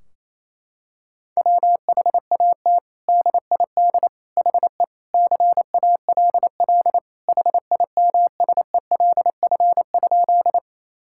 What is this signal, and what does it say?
Morse code, heard as a keyed tone, about 28 wpm, 700 Hz. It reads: WHAT DID HE CALL HIMSELF?
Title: Sentences from Top 1600 Words 28wpm